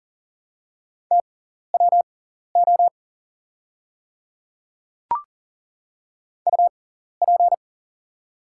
Morse code, heard as a keyed tone, about 40 wpm, spaced as for 14 wpm, 690 Hz. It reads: TWO UP
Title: 6x Character Spacing - Top 200 Words - 40wpm